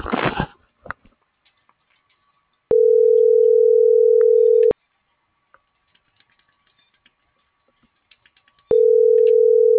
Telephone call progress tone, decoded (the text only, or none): ringback tone